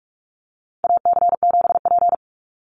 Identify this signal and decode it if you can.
Morse code, heard as a keyed tone, about 44 wpm, 700 Hz. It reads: AC7P
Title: NA SOTA Chaser Callsigns 45wpm